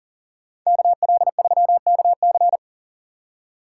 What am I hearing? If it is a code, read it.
Morse code, heard as a keyed tone, about 50 wpm, 700 Hz. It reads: KL3KC
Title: Speed Racing - USA 2x2 Call Signs 40wpm